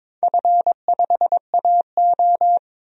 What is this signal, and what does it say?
Morse code, heard as a keyed tone, about 22 wpm, 700 Hz. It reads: F5AO
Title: DXCC-100 Call Signs - 4 Characters - 22wpm